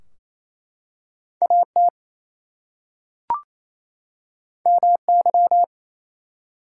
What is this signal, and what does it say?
Morse code, heard as a keyed tone, about 27 wpm, 700 Hz. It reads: AT MY
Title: Two Letter Words - 28wpm